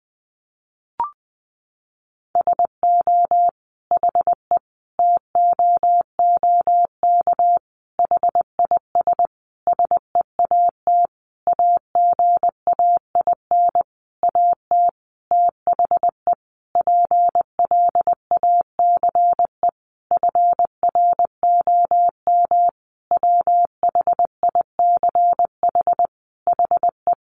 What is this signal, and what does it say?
Morse code, heard as a keyed tone, about 20 wpm, 700 Hz. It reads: SO HE TOOK HIS SEAT AGAIN AT THE PLACE FROM WHICH HE
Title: Sentences from Top 1500 Words 20wpm